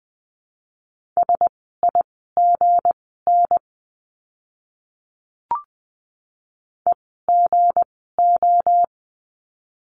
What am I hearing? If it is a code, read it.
Morse code, heard as a keyed tone, about 20 wpm, spaced as for 15 wpm, 700 Hz. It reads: SIGN EGO
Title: 160 Lesson 26f - Letter g - Words 20wpm